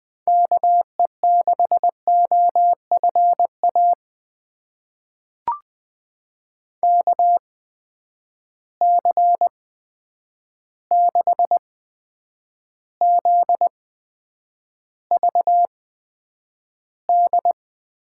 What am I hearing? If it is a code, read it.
Morse code, heard as a keyed tone, about 20 wpm, spaced as for 9 wpm, 700 Hz. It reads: KE6OFA KC6ZVD
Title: USA Call Signs - 6 Characters - 8x Character Spacing - 20wpm